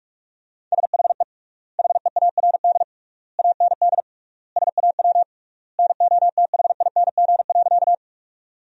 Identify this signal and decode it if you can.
Morse code, heard as a keyed tone, about 42 wpm, 700 Hz. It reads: SHE HEARD AND SAW NOTHING.
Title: 3x Word-Spacing - Sentences from Top 500 Words - 45wpm